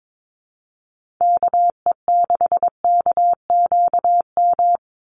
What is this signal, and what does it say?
Morse code, heard as a keyed tone, about 22 wpm, 700 Hz. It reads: KE6KQM
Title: USA Call Signs 22wpm